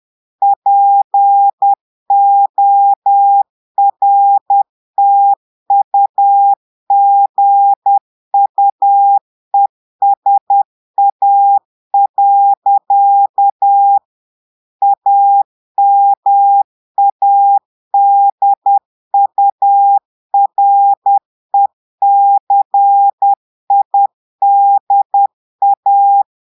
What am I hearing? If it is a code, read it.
Morse code, heard as a keyed tone, about 10 wpm, 800 Hz. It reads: PORTUGUESA. AMADURECIDA
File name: Portugais - texte en clair 1 - vitesse - 10 mots minute